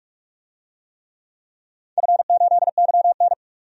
Morse code, heard as a keyed tone, about 46 wpm, 700 Hz. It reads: F8YN
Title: DXCC-100 Call Signs - 4 Characters - 45wpm